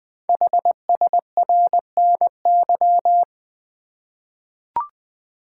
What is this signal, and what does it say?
Morse code, heard as a keyed tone, about 20 wpm, 700 Hz. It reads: HSRNY
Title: Code Groups - 5 Letters - 20wpm